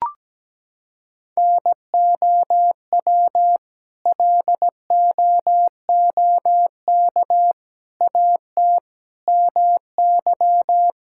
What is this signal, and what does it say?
Morse code, heard as a keyed tone, about 17 wpm, 700 Hz. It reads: NOW LOOK AT MY
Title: Sentences from Top 1600 Words 17wpm